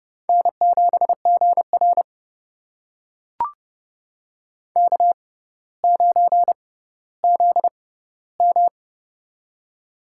Morse code, heard as a keyed tone, about 29 wpm, spaced as for 15 wpm, 700 Hz. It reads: N7GR K9ZM
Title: USA Call Signs - 4 Characters - 6x Character Spacing - 30wpm